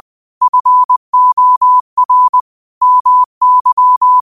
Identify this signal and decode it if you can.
Morse code, heard as a keyed tone, about 20 wpm, 1000 Hz. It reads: FOR MY
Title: 20 WPM morse code quotes for Thu, 29 Feb 2024